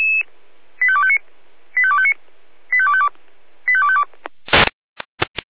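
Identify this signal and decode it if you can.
Five-tone selective call (ZVEI), heard as quick runs of five tones, tones of about 70 ms. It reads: 85259 85259 85252 85252
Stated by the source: ZVEI five-tone call received by SDR, Feuerwehr Orlamünde.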